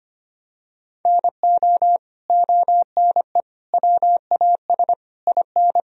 Morse code, heard as a keyed tone, about 25 wpm, 700 Hz. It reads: NO ONE WAS IN
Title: Sentences from Top 1000 Words - Review Part 4 - 25wpm